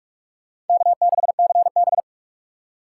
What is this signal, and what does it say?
Morse code, heard as a keyed tone, about 40 wpm, 700 Hz. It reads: K6CB